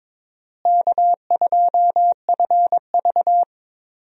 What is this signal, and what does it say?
Morse code, heard as a keyed tone, about 22 wpm, 700 Hz. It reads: K2FV
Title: USA Call Signs - 4 Characters - Repeated 3x - 22wpm